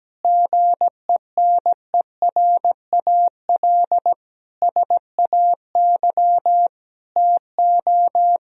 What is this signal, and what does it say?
Morse code, heard as a keyed tone, about 17 wpm, 700 Hz. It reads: GENERAL SAY TO
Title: Sentences from Top 1000 Words - Review Part 3 - 17wpm